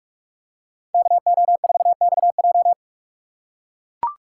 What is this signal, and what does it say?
Morse code, heard as a keyed tone, about 44 wpm, 700 Hz. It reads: KO4XJ